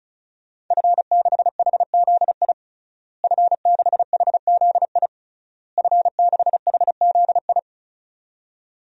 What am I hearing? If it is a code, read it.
Morse code, heard as a keyed tone, about 36 wpm, 700 Hz. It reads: F6HZI F6HZI F6HZI
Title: DXCC-100 Call Signs - 5 Characters - Repeated 3x - 35wpm